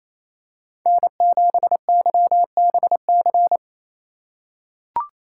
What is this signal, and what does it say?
Morse code, heard as a keyed tone, about 27 wpm, 700 Hz. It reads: N7YBC
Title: USA Call Signs - 5 Characters - 8x Character Spacing - 28wpm